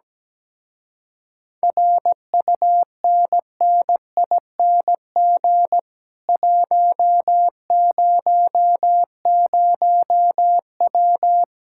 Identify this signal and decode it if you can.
Morse code, heard as a keyed tone, about 17 wpm, 700 Hz. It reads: RUNNING 100W
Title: Speed-Racing Common QSO Phrases Encore - 17wpm